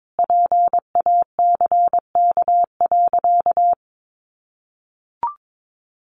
Morse code, heard as a keyed tone, about 22 wpm, 700 Hz. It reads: PACK.
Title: Sentences from Top 1300 Words 22wpm